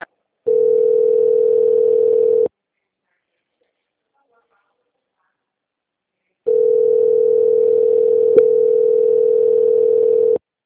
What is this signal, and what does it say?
Ringback tone (North America), 440 + 480 Hz.
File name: out- -2018- - - .8